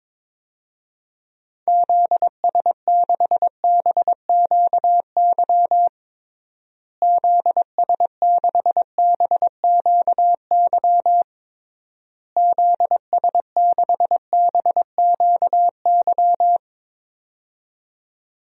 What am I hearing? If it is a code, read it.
Morse code, heard as a keyed tone, about 22 wpm, 700 Hz. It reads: ZS6BQY ZS6BQY ZS6BQY
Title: DXCC-100 Call Signs - 6 Characters - Repeated 3x - 22wpm